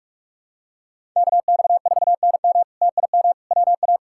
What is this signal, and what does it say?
Morse code, heard as a keyed tone, about 47 wpm, 700 Hz. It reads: KX4NM TIM WA
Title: K1USN SST QSO Practice - 45wpm